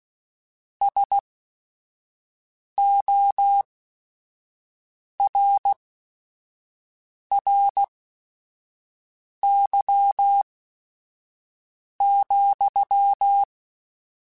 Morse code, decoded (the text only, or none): SORRY,